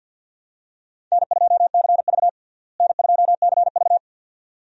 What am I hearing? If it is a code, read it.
Morse code, heard as a keyed tone, about 52 wpm, 700 Hz. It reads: N2CV N2CV
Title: USA Call Signs - 4 Characters - Repeated 2x - 50wpm